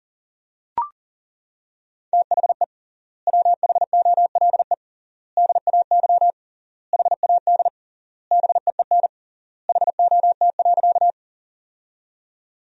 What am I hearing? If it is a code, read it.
Morse code, heard as a keyed tone, about 40 wpm, 700 Hz. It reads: THE WHOLE DAY HAD BEEN HOT.
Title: 3x Word-Spacing - Sentences from Top 500 Words - 40wpm